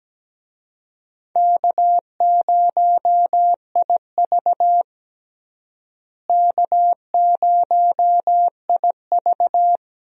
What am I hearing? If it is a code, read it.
Morse code, heard as a keyed tone, about 17 wpm, 700 Hz. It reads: K0IV K0IV
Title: USA Call Signs - 4 Characters - Repeated 2x - 17wpm